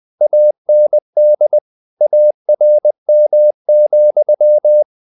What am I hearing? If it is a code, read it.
Morse code, heard as a keyed tone, about 20 wpm, 600 Hz. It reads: AND ARM,
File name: the old man and the sea part 2 20wpm